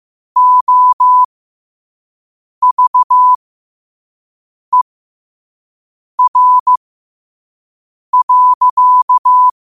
Morse code, heard as a keyed tone, about 15 wpm, spaced as for 5 wpm, 1000 Hz. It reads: OVER.
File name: qotdfiles / 5wpm